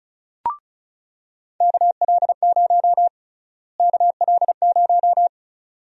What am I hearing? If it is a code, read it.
Morse code, heard as a keyed tone, about 34 wpm, 700 Hz. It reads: KL0 KL0
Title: USA Call Sign Prefixes - Repeated 2x - 35wpm